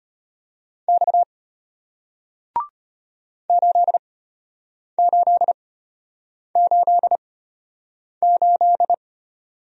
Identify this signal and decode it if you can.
Morse code, heard as a keyed tone, about 34 wpm, spaced as for 8 wpm, 700 Hz. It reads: X 8888